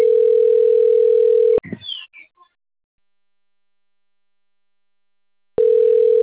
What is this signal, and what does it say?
Ringback tone (North America), 440 + 480 Hz.